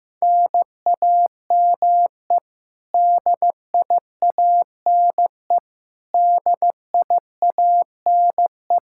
Morse code, heard as a keyed tone, about 16 wpm, 700 Hz. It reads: NAME DIANE DIANE
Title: Practice QSO - Exchange 1 - Beginning - 15wpm